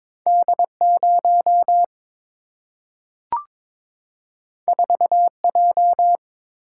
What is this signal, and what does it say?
Morse code, heard as a keyed tone, about 22 wpm, 700 Hz. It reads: D0 4J
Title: Single-Letter Single-Number 22wpm